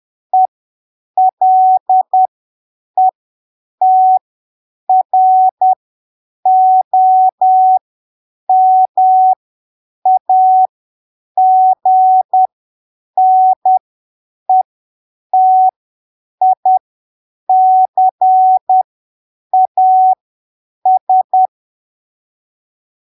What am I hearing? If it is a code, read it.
Morse code, heard as a keyed tone, about 10 wpm, spaced as for 7 wpm, 750 Hz. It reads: ELETROMAGNETICAS
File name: arquivos / texto / texto 10 10ppm